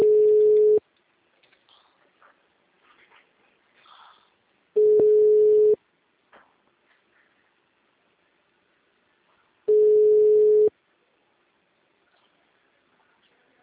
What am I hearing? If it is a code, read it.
Ringback tone (Europe, 425 Hz).